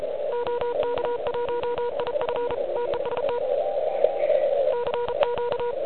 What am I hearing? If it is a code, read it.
Morse code, heard as a keyed tone, about 34 wpm, 960 Hz, with noise. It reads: OK1IF TEST CQ
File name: ARCHIV / QCX / QCX Pileup